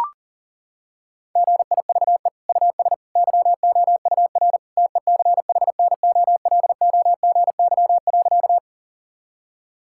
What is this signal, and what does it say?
Morse code, heard as a keyed tone, about 40 wpm, 700 Hz. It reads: GIVE US YOUR TECHNOLOGY.
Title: Sentences from Top 1000 Words - Review Part 5 - 40wpm